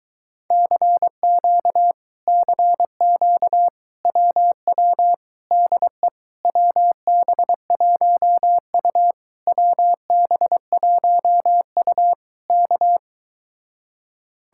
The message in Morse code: CQ CQ WW DE WB1U WB1U K

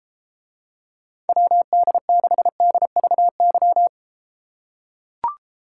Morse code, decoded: WD6DVY

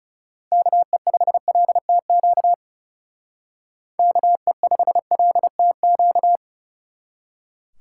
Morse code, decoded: KE5LTQ KE5LTQ

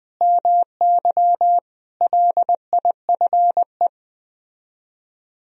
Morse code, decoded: MY LIFE

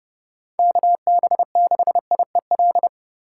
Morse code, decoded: KB6IEL

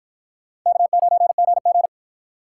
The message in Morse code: K9GG

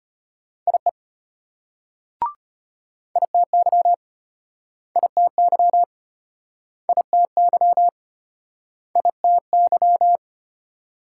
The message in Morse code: IE ITY ITY ITY ITY

